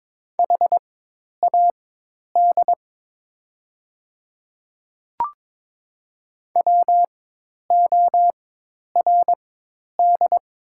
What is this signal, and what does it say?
Morse code, heard as a keyed tone, about 22 wpm, spaced as for 10 wpm, 700 Hz. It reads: HAD WORD
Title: 4x Character Spacing - Top 100 Words - 22wpm